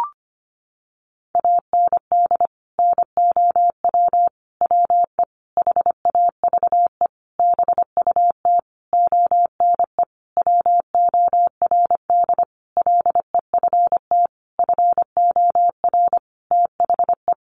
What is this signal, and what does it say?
Morse code, heard as a keyed tone, about 25 wpm, 700 Hz. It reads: AND NOW WE HAVE BUT ONE WORD LEFT FOR THE
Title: Sentences from Top 1500 Words 25wpm